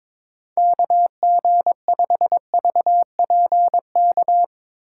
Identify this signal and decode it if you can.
Morse code, heard as a keyed tone, about 22 wpm, 700 Hz. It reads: KG5VPK
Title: USA Call Signs - 2x Character Spacing - 22wpm